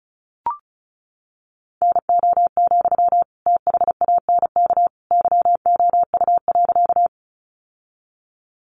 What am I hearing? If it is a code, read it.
Morse code, heard as a keyed tone, about 34 wpm, 700 Hz. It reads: NO, THANK YOU.
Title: Sentences from Top 600 Words 35wpm